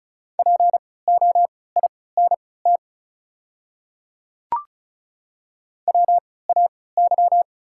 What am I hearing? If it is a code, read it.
Morse code, heard as a keyed tone, about 36 wpm, spaced as for 20 wpm, 690 Hz. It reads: POINT WAY
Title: 3x Character Spacing - Top 300 Words - 35wpm